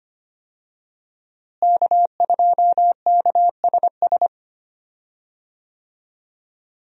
Morse code, heard as a keyed tone, about 25 wpm, 700 Hz. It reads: K2KSS
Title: USA Call Signs - 5 Characters - Repeated 2x - 25wpm